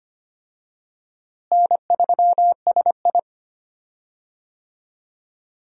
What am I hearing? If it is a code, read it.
Morse code, heard as a keyed tone, about 25 wpm, 700 Hz. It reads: N3SI
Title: USA Call Signs - 4 Characters - Repeated 2x - 25wpm